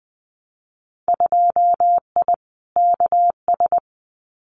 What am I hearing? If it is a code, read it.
Morse code, heard as a keyed tone, about 20 wpm, 700 Hz. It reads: 2I KS
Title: Winter Field Day Exchanges Encore 20wpm